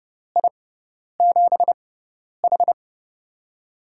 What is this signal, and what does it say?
Morse code, heard as a keyed tone, about 32 wpm, spaced as for 10 wpm, 700 Hz. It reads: I7H
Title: USA Call Signs - 4 Characters - 6x Character Spacing - 30wpm